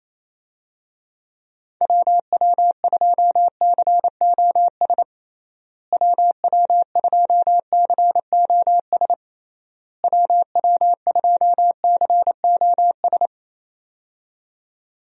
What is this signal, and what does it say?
Morse code, heard as a keyed tone, about 28 wpm, 700 Hz. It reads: WW2COS WW2COS WW2COS
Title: USA Call Signs - 6 Characters - Repeated 3x - 28wpm